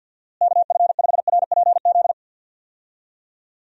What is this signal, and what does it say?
Morse code, heard as a keyed tone, about 47 wpm, 700 Hz. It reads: KF5RPZ
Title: USA Call Signs - 6 Characters - Repeated 2x - 50wpm